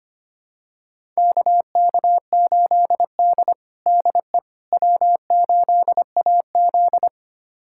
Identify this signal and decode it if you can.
Morse code, heard as a keyed tone, about 25 wpm, 700 Hz. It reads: KK8D DE W8AZ
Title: USA Callsign DE Callsign - 4 Characters - 25wpm